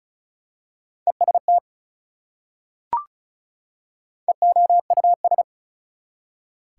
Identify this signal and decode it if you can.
Morse code, heard as a keyed tone, about 36 wpm, 700 Hz. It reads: EST EOUS
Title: Prefixes and Suffixes 35wpm